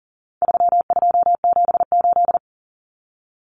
Morse code, heard as a keyed tone, about 40 wpm, 700 Hz. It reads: 3278
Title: Common Cut Numbers - 4-Digit 40wpm